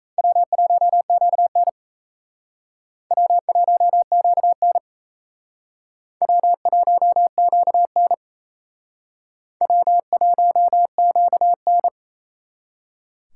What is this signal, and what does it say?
Morse code, heard as a keyed tone, about 35 wpm, 700 Hz. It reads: W1QN W1QN W1QN W1QN
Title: Speed Racing - USA 1x2 Call Signs 28wpm